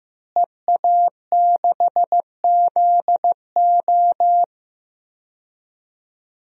Morse code, heard as a keyed tone, about 15 wpm, 700 Hz. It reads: EA6ZO
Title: DXCC-100 Call Signs - 5 Characters - Repeated 3x - 15wpm